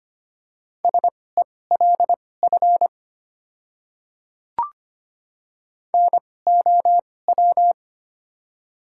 Morse code, heard as a keyed tone, about 25 wpm, spaced as for 18 wpm, 700 Hz. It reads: SELF NOW
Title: 2x Character Spacing - Top 300 Words - 25wpm